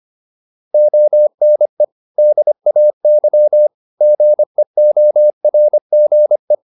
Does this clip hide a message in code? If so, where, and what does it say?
Yes: Morse code from 0.7 to 6.5 s, heard as a keyed tone, about 25 wpm, 600 Hz. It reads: ONE DAY GEORGE